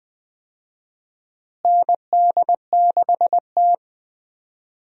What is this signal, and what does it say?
Morse code, heard as a keyed tone, about 20 wpm, 700 Hz. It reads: ND6T
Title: USA Call Signs - 4 Characters - Repeated 2x - 20wpm